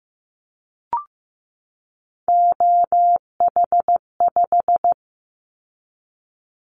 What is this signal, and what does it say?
Morse code, heard as a keyed tone, about 15 wpm, 700 Hz. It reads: OH5
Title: Callsign Prefixes for Top 50 DXCC - 15wpm